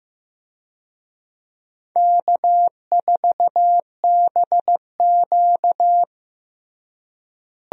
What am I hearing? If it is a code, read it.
Morse code, heard as a keyed tone, about 16 wpm, 700 Hz. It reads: K4BQ
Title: USA Call Signs - 4 Characters - Repeated 2x - 15wpm